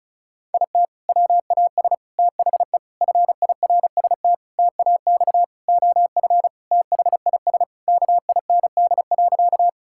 Morse code, heard as a keyed tone, about 36 wpm, 700 Hz. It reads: IT WAS THE FIRST TAX OF THIS KIND.
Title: Sentences from Top 1000 Words - Review Part 5 - 35wpm